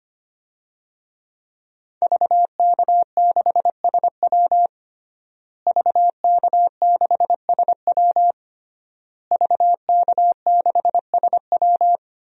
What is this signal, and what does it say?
Morse code, heard as a keyed tone, about 25 wpm, 700 Hz. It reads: VK6SW VK6SW VK6SW
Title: DXCC-100 Call Signs - 5 Characters - Repeated 3x - 25wpm